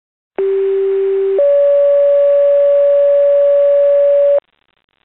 Two-tone page, heard as a one-second tone, then a three-second tone, about 390 then 570 Hz. It is heard from 0.4 to 4.4 s.